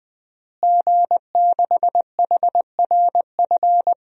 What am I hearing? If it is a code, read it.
Morse code, heard as a keyed tone, about 20 wpm, 700 Hz. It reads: G6HRF
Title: DXCC-100 Call Signs - 5 Characters - Repeated 2x - 20wpm